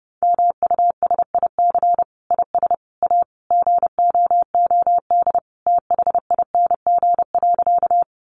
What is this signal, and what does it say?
Morse code, heard as a keyed tone, about 29 wpm, 700 Hz. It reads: MUSIC IS A GOOD THING.
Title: Sentences from Top 300 Words 30wpm